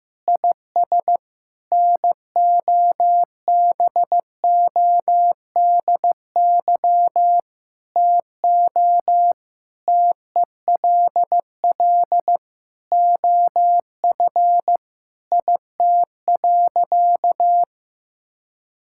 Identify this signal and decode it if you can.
Morse code, heard as a keyed tone, about 15 wpm, 700 Hz. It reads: IS NOBODY TO TELL OF IT.
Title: Sentences from Top 1600 Words 15wpm